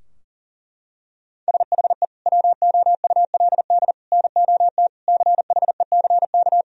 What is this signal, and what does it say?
Morse code, heard as a keyed tone, about 40 wpm, 700 Hz. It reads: SHE WOULD NOT CHECK
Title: Sentences from Top 1000 Words - Review Part 3 - 40wpm